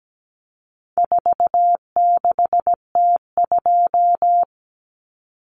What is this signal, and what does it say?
Morse code, heard as a keyed tone, about 17 wpm, 700 Hz. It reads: 46T2
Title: Common Cut Numbers - 4-Digit 17wpm